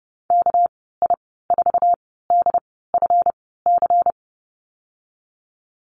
Morse code, heard as a keyed tone, about 32 wpm, spaced as for 17 wpm, 700 Hz. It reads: KI4DFC
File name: Lesson 37g - Letter x - Call Signs 30wpm